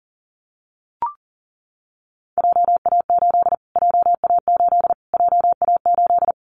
Morse code, heard as a keyed tone, about 37 wpm, 700 Hz. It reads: JA8 JA8 JA8
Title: Callsign Prefixes for Top 25 DXCC - Repeated 3x - 40wpm